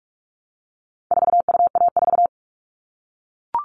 Morse code, heard as a keyed tone, about 44 wpm, 700 Hz. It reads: 4UA4
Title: All Cut Numbers - 4-Digit 45wpm